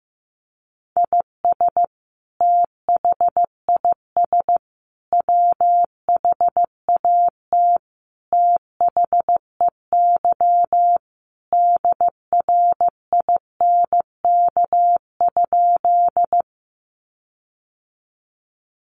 Morse code, heard as a keyed tone, about 15 wpm, 700 Hz. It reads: IS THIS WHAT THEY DRINK?
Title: Sentences from Top 1500 Words 15wpm